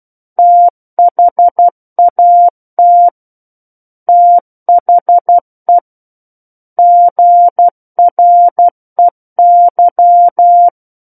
Morse code, heard as a keyed tone, about 12 wpm, 700 Hz. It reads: THAT THE GREY